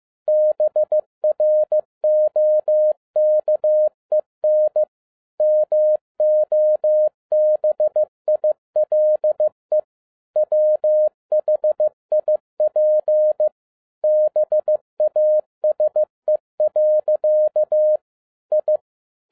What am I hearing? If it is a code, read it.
Morse code, heard as a keyed tone, about 15 wpm, 610 Hz. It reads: BROKEN MOBILE WHIP BASE. I
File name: 2012-01-mp3 / n07-ARA JR0DIL-15wpm